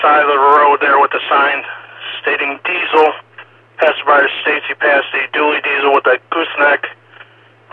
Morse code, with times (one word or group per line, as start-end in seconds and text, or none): none